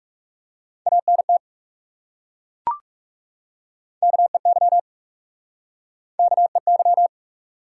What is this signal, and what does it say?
Morse code, heard as a keyed tone, about 40 wpm, 700 Hz. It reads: ANT KEY KEY